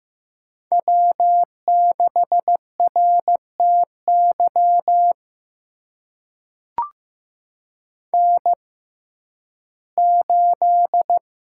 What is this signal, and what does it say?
Morse code, heard as a keyed tone, about 15 wpm, spaced as for 11 wpm, 700 Hz. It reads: W6RTY N8
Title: USA Call Signs - 5 Characters - 6x Character Spacing - 15wpm